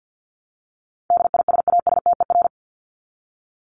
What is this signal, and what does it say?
Morse code, heard as a keyed tone, about 50 wpm, 700 Hz. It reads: DISASTER
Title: Top 1501-1750 Words 50wpm